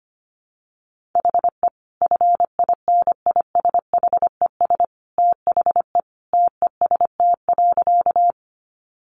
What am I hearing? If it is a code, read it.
Morse code, heard as a keyed tone, about 25 wpm, 700 Hz. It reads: HE FINISHES THE TEST.
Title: Additional Easy Phrases 25wpm